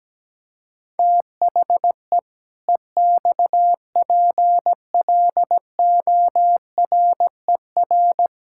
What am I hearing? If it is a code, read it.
Morse code, heard as a keyed tone, about 17 wpm, 700 Hz. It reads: THE EXPLORER
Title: Pangram Challenge v2 - 17wpm